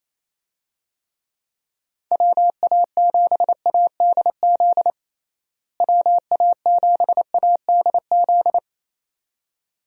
Morse code, heard as a keyed tone, about 28 wpm, 700 Hz. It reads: WA7ADZ WA7ADZ
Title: USA Call Signs - 6 Characters - Repeated 2x - 28wpm